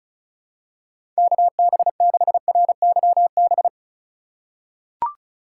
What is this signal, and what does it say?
Morse code, heard as a keyed tone, about 34 wpm, 700 Hz. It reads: KB6RYB
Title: USA Call Signs - 6 Characters - 4x Character Spacing - 35wpm